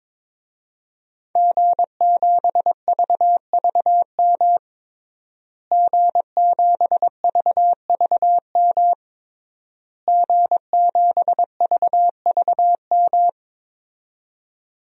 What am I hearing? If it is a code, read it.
Morse code, heard as a keyed tone, about 22 wpm, 700 Hz. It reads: G7VVM G7VVM G7VVM
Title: DXCC-100 Call Signs - 5 Characters - Repeated 3x - 22wpm